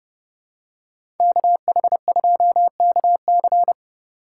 Morse code, heard as a keyed tone, about 30 wpm, 700 Hz. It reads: KH2KC
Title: USA Call Signs - 5 Characters - 2x Character Spacing - 30wpm